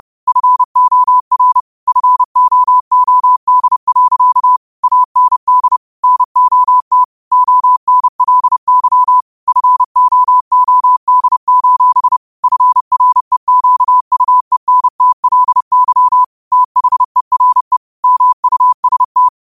Morse code, heard as a keyed tone, about 30 wpm, 1000 Hz. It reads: FOR FOOD. AND NOT ONLY FOOD: FREQUENTLY THERE MUST
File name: qotdfiles / 30wpm